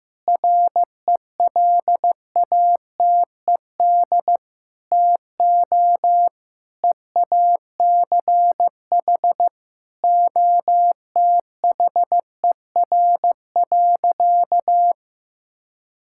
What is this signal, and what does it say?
Morse code, heard as a keyed tone, about 15 wpm, 700 Hz. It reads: RELATED TO EACH OTHER.
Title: Sentences from Top 1000 Words - Review Part 5 - 15wpm